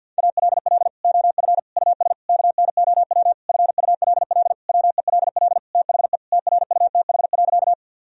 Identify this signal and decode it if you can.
Morse code, heard as a keyed tone, about 50 wpm, 700 Hz. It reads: ALL OF US KNOW FULL WELL THE TRUTH.
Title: Sentences from Top 1000 Words - Review Part 3 - 50wpm